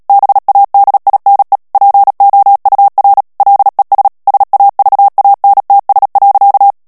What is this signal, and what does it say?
Morse code, heard as a keyed tone, about 37 wpm, 800 Hz. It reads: BADINE POUR LES SAVANTS.
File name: CW / FICHIERS-MP3 / FRANCAIS TEXTE CLAIR 2 VITESSE 35 MOTS MINUTES